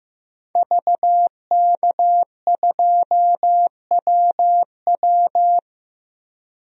Morse code, heard as a keyed tone, about 15 wpm, 700 Hz. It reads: VK2WW
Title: DXCC-100 Call Signs - 5 Characters - 15wpm